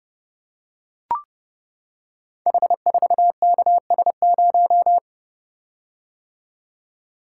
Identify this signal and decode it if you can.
Morse code, heard as a keyed tone, about 30 wpm, 700 Hz. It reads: H4KS0
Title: Code Groups - 5 Letter:Numbers - 30wpm